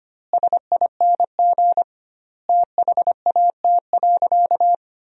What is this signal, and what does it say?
Morse code, heard as a keyed tone, about 25 wpm, 700 Hz. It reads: SING THAT.
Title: 2x Word-Spacing - Sentences from Top 500 Words - 25wpm